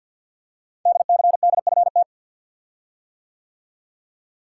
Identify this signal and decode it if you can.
Morse code, heard as a keyed tone, about 52 wpm, 700 Hz. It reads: DCDFT
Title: Code Groups - 5 Letters - 50wpm